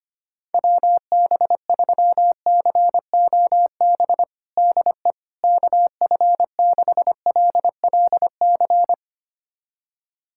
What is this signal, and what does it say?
Morse code, heard as a keyed tone, about 25 wpm, 700 Hz. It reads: WB3COB DE KF6LLC
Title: USA Callsign DE Callsign - 6 Characters - 25wpm